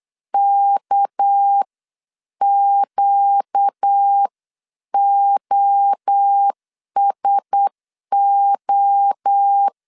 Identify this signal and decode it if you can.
Morse code, heard as a keyed tone, about 8 wpm, spaced as for 7 wpm, 800 Hz. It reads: KQOSO
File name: CW / DET / 1 9 1500C S A 8wpm 5637